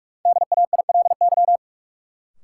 Morse code, heard as a keyed tone, about 47 wpm, 700 Hz. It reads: DAILY
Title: Sentences from Top 1100 Words - New Word Drill - 45wpm